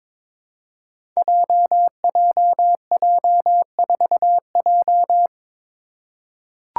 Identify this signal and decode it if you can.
Morse code, heard as a keyed tone, about 21 wpm, 700 Hz. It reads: JJJ4J
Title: Five Challenging Letter-Number Combinations 22wpm